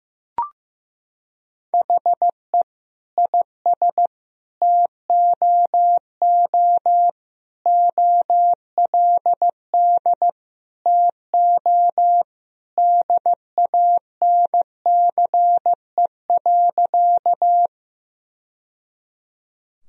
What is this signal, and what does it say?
Morse code, heard as a keyed tone, about 15 wpm, 700 Hz. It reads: HE IS TOO OLD TO DANCE.
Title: Sentences from Top 1000 Words - Review Part 4 - 15wpm